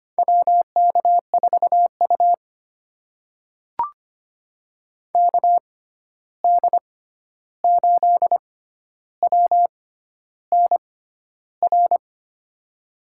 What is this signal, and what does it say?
Morse code, heard as a keyed tone, about 25 wpm, spaced as for 11 wpm, 700 Hz. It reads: WK4U KD8WNR
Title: USA Call Signs - 6x Character Spacing - 25wpm